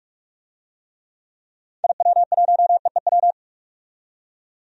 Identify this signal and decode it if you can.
Morse code, heard as a keyed tone, about 43 wpm, 700 Hz. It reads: IW1EEW
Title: DXCC-100 Call Signs - 6 Characters - 45wpm